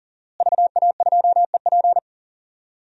Morse code, heard as a keyed tone, about 40 wpm, 700 Hz. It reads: VA2EP
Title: DXCC-100 Call Signs - 5 Characters - Repeated 2x - 40wpm